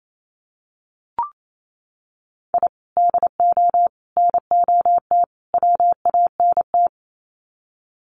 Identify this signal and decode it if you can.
Morse code, heard as a keyed tone, about 28 wpm, 700 Hz. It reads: I DO NOT WANT
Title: Sets of 4 Words 28wpm